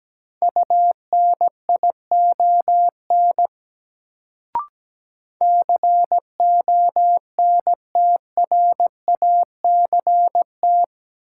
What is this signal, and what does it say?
Morse code, heard as a keyed tone, about 17 wpm, 700 Hz. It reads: UNION CONTRACT
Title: Sentences from Top 1400 Words - New Word Drill Alt - 17wpm